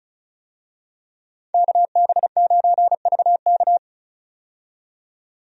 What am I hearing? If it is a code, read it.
Morse code, heard as a keyed tone, about 36 wpm, 700 Hz. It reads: KB9VK
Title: USA Call Signs - 5 Characters - 35wpm